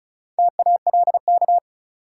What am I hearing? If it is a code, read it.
Morse code, heard as a keyed tone, about 36 wpm, 700 Hz. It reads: TALK
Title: Sentences from Top 400 Words - New Word Drill - 35wpm